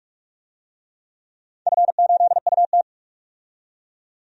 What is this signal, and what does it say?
Morse code, heard as a keyed tone, about 41 wpm, 700 Hz. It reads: F8UT
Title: DXCC-100 Call Signs - 4 Characters - 45wpm